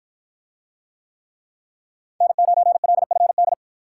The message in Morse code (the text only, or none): N9LFD